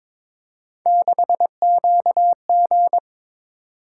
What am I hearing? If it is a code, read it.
Morse code, heard as a keyed tone, about 22 wpm, 700 Hz. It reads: 6QG
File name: practice / 22wpm / Three Letter-Number Combinations - 22wpm